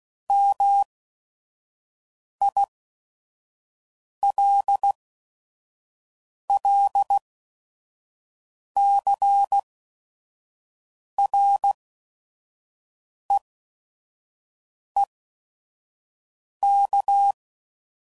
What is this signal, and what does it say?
Morse code, heard as a keyed tone, about 16 wpm, spaced as for 5 wpm, 800 Hz. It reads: MILLCREEK
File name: CW Code Groups @5WPM / fromSM's / Cities